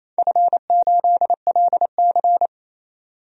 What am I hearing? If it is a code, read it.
Morse code, heard as a keyed tone, about 28 wpm, 700 Hz. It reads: F8LC
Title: DXCC-100 Call Signs - 4 Characters - 28wpm